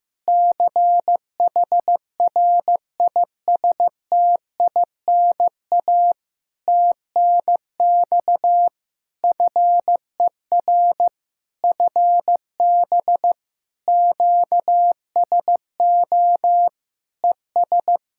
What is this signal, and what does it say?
Morse code, heard as a keyed tone, about 15 wpm, 700 Hz. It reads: CHRISTINA TNX FER FB QSO ES
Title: Practice QSO - Exchange 4 - Ending - 15wpm